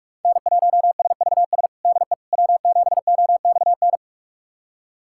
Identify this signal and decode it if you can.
Morse code, heard as a keyed tone, about 41 wpm, 700 Hz. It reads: N1SVS DE W7OXN
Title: USA Callsign DE Callsign - 5 Characters - 45wpm